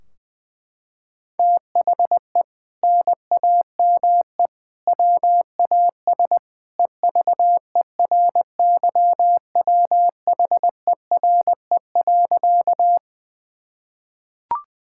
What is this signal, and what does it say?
Morse code, heard as a keyed tone, about 20 wpm, 700 Hz. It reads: THE NAME WAS EVERYWHERE.
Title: Sentences from Top 1600 Words 20wpm